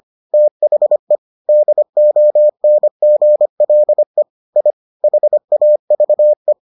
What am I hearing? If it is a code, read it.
Morse code, heard as a keyed tone, about 25 wpm, 600 Hz. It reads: THE DONGLE I HAVE